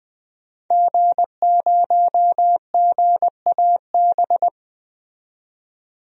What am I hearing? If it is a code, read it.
Morse code, heard as a keyed tone, about 21 wpm, 700 Hz. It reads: G0GAB
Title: DXCC-100 Call Signs - 5 Characters - 20wpm